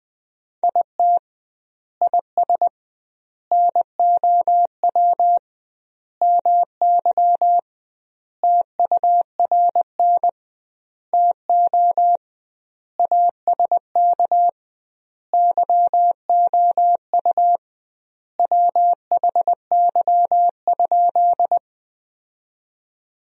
Morse code, heard as a keyed tone, about 21 wpm, 700 Hz. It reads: IT IS NOW MY TURN TO ASK YOU WHY?
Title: 2x Word-Spacing - Sentences from Top 200 Words - 20wpm